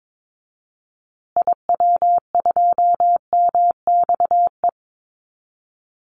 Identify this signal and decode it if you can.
Morse code, heard as a keyed tone, about 22 wpm, 700 Hz. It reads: IW2MXE
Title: Call Signs Repeated 2x - 22wpm